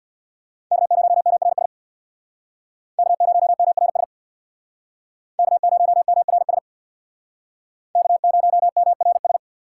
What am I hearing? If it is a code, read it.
Morse code, heard as a keyed tone, about 66 wpm, 700 Hz. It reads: K0MRS K0MRS K0MRS K0MRS